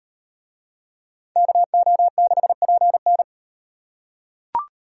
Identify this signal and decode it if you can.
Morse code, heard as a keyed tone, about 34 wpm, 700 Hz. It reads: KO6PN